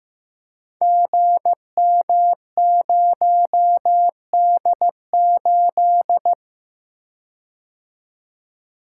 Morse code, heard as a keyed tone, about 15 wpm, 700 Hz. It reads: GM0D8
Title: Code Groups - 5 Letter:Numbers - 15wpm